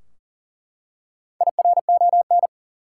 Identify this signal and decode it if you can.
Morse code, heard as a keyed tone, about 38 wpm, 700 Hz. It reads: IRON
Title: Four Letter Words - 40wpm